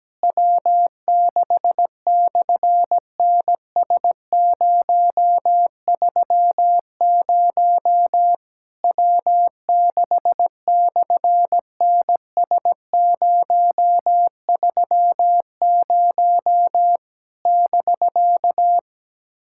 Morse code, heard as a keyed tone, about 17 wpm, 700 Hz. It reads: W6/NS030 W6/NS030 <BK>
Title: SOTA QSO Practice 17wpm